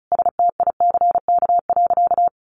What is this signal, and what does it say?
Morse code, heard as a keyed tone, about 34 wpm, 700 Hz. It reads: STICK.
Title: Sentences from Top 1200 Words 35wpm